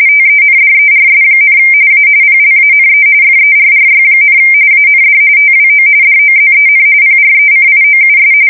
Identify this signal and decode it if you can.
RTTY, about 45.45 baud. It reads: TENDERS, A PLANE FERRY, AND TWO DOZEN PRECIOUS
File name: fsk170 / battleshipnj